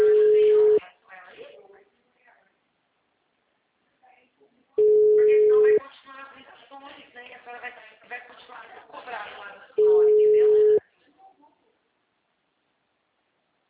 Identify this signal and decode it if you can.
Ringback tone (Europe, 425 Hz), with speech.